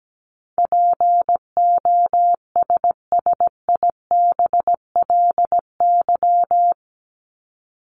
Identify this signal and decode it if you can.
Morse code, heard as a keyed tone, about 17 wpm, 700 Hz. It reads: POSSIBLY
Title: LY Suffix - Top 100 Words with Base Words 17wpm